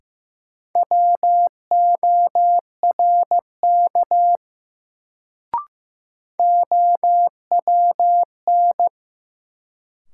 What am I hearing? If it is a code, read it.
Morse code, heard as a keyed tone, about 15 wpm, 700 Hz. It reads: WORK OWN
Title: Sets of 2 Words - New Word Drill - 15wpm